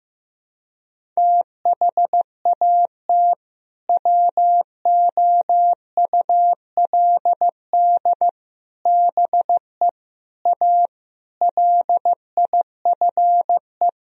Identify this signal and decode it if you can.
Morse code, heard as a keyed tone, about 16 wpm, 700 Hz. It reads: THAT WOULD BE A LIFE
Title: Sentences from Top 1000 Words - Review Part 4 - 15wpm